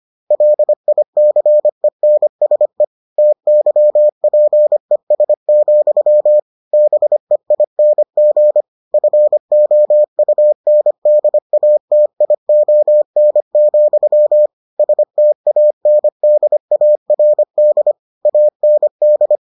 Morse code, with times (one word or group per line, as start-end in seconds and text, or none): 0.3-2.8 LICENSE
3.2-6.4 TYPES,
6.7-8.6 BEING
8.9-14.5 FOUNDATION,
14.8-17.9 STANDARD
18.3-19.4 AND